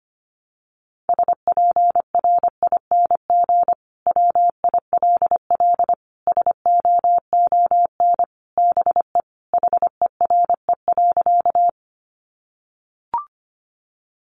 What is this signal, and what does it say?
Morse code, heard as a keyed tone, about 25 wpm, 700 Hz. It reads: SPRING WILL SOON BE HERE.
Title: Sentences from Top 1200 Words 25wpm